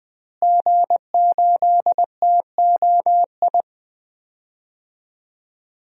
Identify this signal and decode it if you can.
Morse code, heard as a keyed tone, about 20 wpm, 700 Hz. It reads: G8TOI
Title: DXCC-100 Call Signs - 5 Characters - Repeated 3x - 20wpm